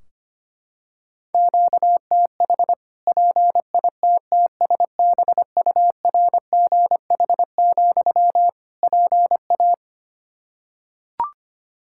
Morse code, heard as a keyed tone, about 25 wpm, 700 Hz. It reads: QTH PITTSBURGH, PA
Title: Practice QSO - Exchange 1 - Beginning - 25wpm